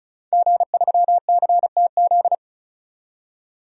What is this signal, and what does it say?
Morse code, heard as a keyed tone, about 36 wpm, 700 Hz. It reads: G3CTZ
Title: DXCC-100 Call Signs - 5 Characters - 35wpm